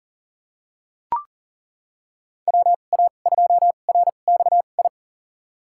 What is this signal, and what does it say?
Morse code, heard as a keyed tone, about 40 wpm, spaced as for 29 wpm, 700 Hz. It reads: WA2RXI
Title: USA Call Signs - 6 Characters - 2x Character Spacing - 40wpm